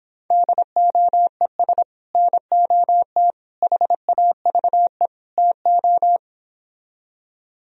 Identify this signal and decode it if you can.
Morse code, heard as a keyed tone, about 26 wpm, 700 Hz. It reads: DOES NOT HAVE TO
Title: Speed-Racing - Sets of 4 Words 22wpm